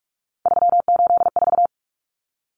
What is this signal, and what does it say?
Morse code, heard as a keyed tone, about 44 wpm, 700 Hz. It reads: 384
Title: Common Cut Numbers - 3-Digit 45wpm